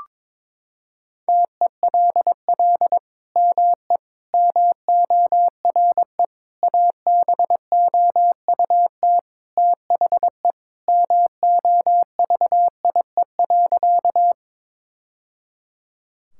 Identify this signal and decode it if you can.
Morse code, heard as a keyed tone, about 22 wpm, 700 Hz. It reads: TELL ME MORE ABOUT THE MOVIE.
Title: Sentences from Top 1000 Words - Review Part 5 - 22wpm